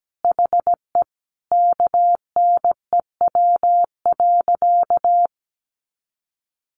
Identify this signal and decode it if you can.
Morse code, heard as a keyed tone, about 17 wpm, 700 Hz. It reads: HE KNEW.
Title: Sentences from Top 400 Words 17wpm